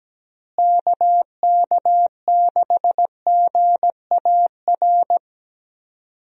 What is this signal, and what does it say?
Morse code, heard as a keyed tone, about 17 wpm, 700 Hz. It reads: KK6GAR
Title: USA Call Signs - 6 Characters - Repeated 2x - 17wpm